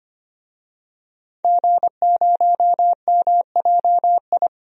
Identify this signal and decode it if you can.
Morse code, heard as a keyed tone, about 25 wpm, 700 Hz. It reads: G0MJI
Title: DXCC-100 Call Signs - 5 Characters - 25wpm